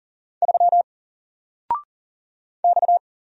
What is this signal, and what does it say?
Morse code, heard as a keyed tone, about 39 wpm, 700 Hz. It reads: 3 X
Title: Single Letter-Number 40wpm